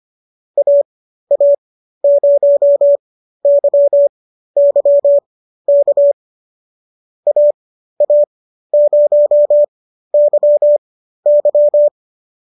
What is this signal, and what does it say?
Morse code, heard as a keyed tone, about 25 wpm, spaced as for 13 wpm, 570 Hz. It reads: AA0YY K AA0YY
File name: qso213 13